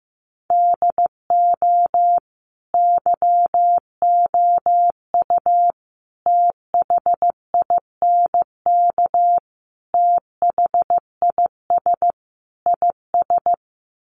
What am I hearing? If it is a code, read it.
Morse code, heard as a keyed tone, about 15 wpm, 700 Hz. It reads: DO YOU THINK THIS IS
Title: Sets of 5 Words Encore2 15wpm